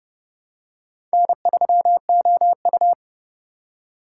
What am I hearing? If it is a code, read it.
Morse code, heard as a keyed tone, about 30 wpm, 700 Hz. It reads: N3OU
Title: USA Call Signs - 4 Characters - 4x Character Spacing - 30wpm